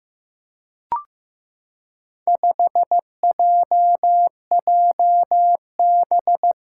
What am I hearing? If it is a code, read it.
Morse code, heard as a keyed tone, about 15 wpm, 700 Hz. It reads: 5JJB
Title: Four Challenging Letter-Number Combinations 15wpm